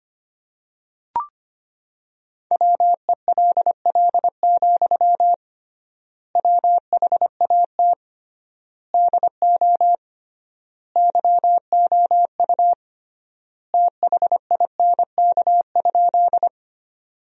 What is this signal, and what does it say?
Morse code, heard as a keyed tone, about 25 wpm, 700 Hz. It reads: WELL, WHAT DO YOU THINK?
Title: 3x Word-Spacing - Sentences from Top 200 Words - 25wpm